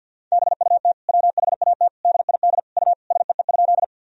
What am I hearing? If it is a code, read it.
Morse code, heard as a keyed tone, about 50 wpm, 700 Hz. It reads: BUT WHAT DID U SEE?
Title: Sentences from Top 100 Words and Abbreviations 50wpm